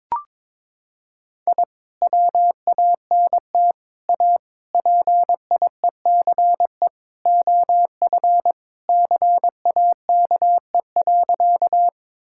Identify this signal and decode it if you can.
Morse code, heard as a keyed tone, about 22 wpm, 700 Hz. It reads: I WANT A PIECE OF CAKE.
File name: practice / 22wpm / Sentences with 3-Word Phrases - 22wpm